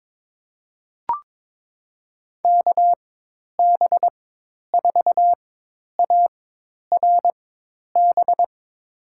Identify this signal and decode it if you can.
Morse code, heard as a keyed tone, about 22 wpm, spaced as for 10 wpm, 700 Hz. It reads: KB4ARB
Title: USA Call Signs - 4x Character Spacing - 22wpm